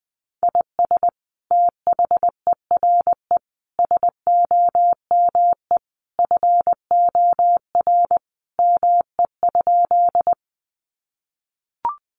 Morse code, heard as a keyed tone, about 20 wpm, 700 Hz. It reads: IS THERE SOME FOR ME?
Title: Sentences from Top 200 Words 20wpm